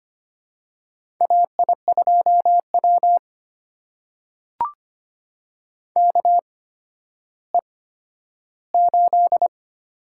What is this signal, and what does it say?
Morse code, heard as a keyed tone, about 25 wpm, spaced as for 12 wpm, 700 Hz. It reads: AI2W KE8
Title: USA Call Signs - 4 Characters - 8x Character Spacing - 25wpm